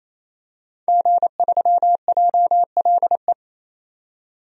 Morse code, heard as a keyed tone, about 28 wpm, 700 Hz. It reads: G3JLE
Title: DXCC-100 Call Signs - 5 Characters - 28wpm